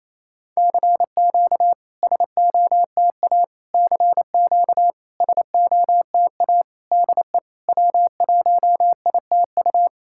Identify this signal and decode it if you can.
Morse code, heard as a keyed tone, about 28 wpm, 690 Hz. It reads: CQ SOTA CQ SOTA DE W1ITU
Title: SOTA QSO Practice 28wpm